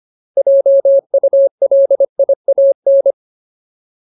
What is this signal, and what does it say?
Morse code, heard as a keyed tone, about 25 wpm, 550 Hz. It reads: JULIAN
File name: Ultimate Word List / 25WPM Ultimate